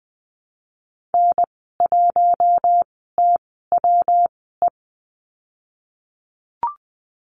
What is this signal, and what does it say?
Morse code, heard as a keyed tone, about 20 wpm, spaced as for 14 wpm, 700 Hz. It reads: N1TWE
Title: 140 Lesson 23g - Letter f - Call Signs 20wpm